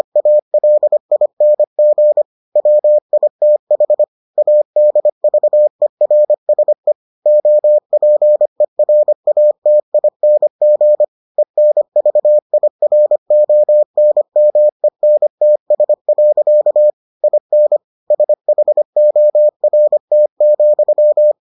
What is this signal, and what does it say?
Morse code, heard as a keyed tone, about 25 wpm, 600 Hz. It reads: ALING WITH ADVERSE OPERATING ENVIRONMENTS. IN SHORT,